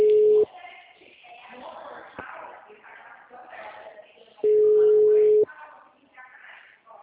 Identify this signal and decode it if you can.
Ringback tone (Europe, 425 Hz), with speech.